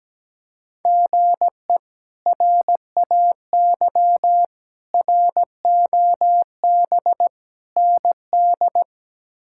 Morse code, heard as a keyed tone, about 17 wpm, 700 Hz. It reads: GE RAY ROB ND
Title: K1USN SST QSO Practice - 17wpm